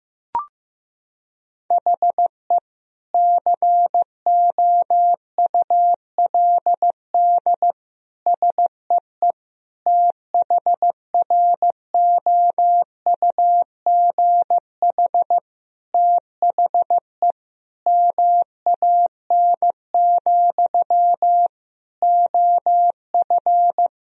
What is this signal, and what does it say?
Morse code, heard as a keyed tone, about 15 wpm, 700 Hz. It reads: HE COULD SEE THROUGH THE MAN, OF
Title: Sentences from Top 1000 Words - Review Part 1 - 15wpm